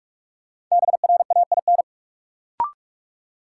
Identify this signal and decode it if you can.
Morse code, heard as a keyed tone, about 44 wpm, 700 Hz. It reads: BRAIN
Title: Sentences from Top 1100 Words - New Word Drill Alt - 45wpm